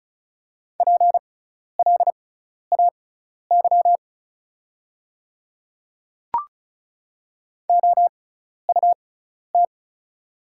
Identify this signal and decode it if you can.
Morse code, heard as a keyed tone, about 36 wpm, spaced as for 12 wpm, 690 Hz. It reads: PLAY OUT
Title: 6x Character Spacing - Top 300 Words - 35wpm